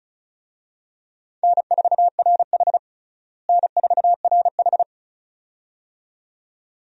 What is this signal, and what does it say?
Morse code, heard as a keyed tone, about 36 wpm, 700 Hz. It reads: N4RH N4RH
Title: USA Call Signs - 4 Characters - Repeated 2x - 35wpm